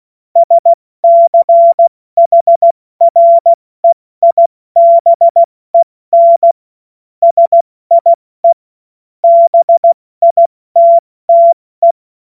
Morse code, heard as a keyed tone, about 16 wpm, 680 Hz. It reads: SCHREIBEN SIE BITTE
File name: Morsekurs / Witze / Witz 22. Juni 2021, 70 F80